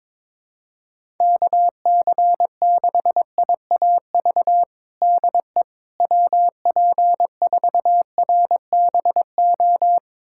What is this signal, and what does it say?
Morse code, heard as a keyed tone, about 21 wpm, 700 Hz. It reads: KC6IAV DE WP4RBO
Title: USA Callsign DE Callsign 22wpm